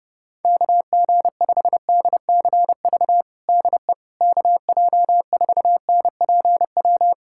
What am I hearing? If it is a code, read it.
Morse code, heard as a keyed tone, about 30 wpm, 700 Hz. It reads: KG5DCV DE KJ4NPW
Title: USA Callsign DE Callsign - 6 Characters - 30wpm